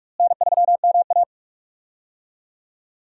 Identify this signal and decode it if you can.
Morse code, heard as a keyed tone, about 44 wpm, 700 Hz. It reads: N3MA
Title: USA Call Signs - 4 Characters - Repeated 3x - 45wpm